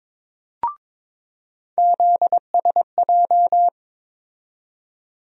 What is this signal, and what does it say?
Morse code, heard as a keyed tone, about 22 wpm, 700 Hz. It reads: ZSJ
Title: USA Call Sign Postfixes - 3 Letters 22wpm